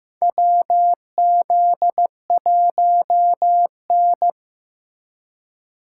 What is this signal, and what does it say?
Morse code, heard as a keyed tone, about 15 wpm, 700 Hz. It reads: WZ1N
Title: USA Call Signs - 4 Characters - 3x Character Spacing - 15wpm